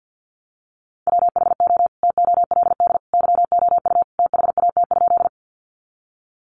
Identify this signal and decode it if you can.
Morse code, heard as a keyed tone, about 50 wpm, 700 Hz. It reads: WHO TOLD YOU THAT?
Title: Sentences from Top 400 Words 50wpm